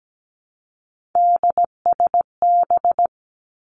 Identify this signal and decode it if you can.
Morse code, heard as a keyed tone, about 17 wpm, 700 Hz. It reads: DSB
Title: Three Challenging Letter-Number Combinations 17wpm